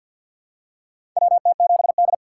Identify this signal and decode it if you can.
Morse code, heard as a keyed tone, about 52 wpm, 700 Hz. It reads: WT7D
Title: USA Call Signs - 4x Character Spacing - 50wpm